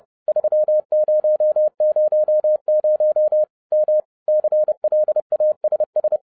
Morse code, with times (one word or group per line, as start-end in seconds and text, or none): 0.3-3.4 3000
3.7-4.0 M
4.3-6.2 CLASS